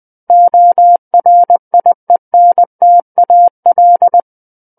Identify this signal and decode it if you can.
Morse code, heard as a keyed tone, about 20 wpm, 700 Hz. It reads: ORIENTAL
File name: 20 WPM